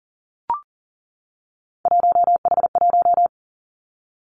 Morse code, heard as a keyed tone, about 40 wpm, 700 Hz. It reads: 1H1
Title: Three Challenging Letter-Number Combinations 40wpm